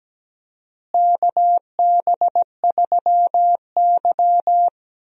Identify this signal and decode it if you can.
Morse code, heard as a keyed tone, about 17 wpm, 700 Hz. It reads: KB3Y
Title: USA Call Signs - 4 Characters - Repeated 3x - 17wpm